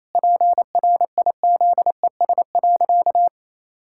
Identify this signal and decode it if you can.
Morse code, heard as a keyed tone, about 28 wpm, 700 Hz. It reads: PRIZES.
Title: Pangram Challenge v2 - 28wpm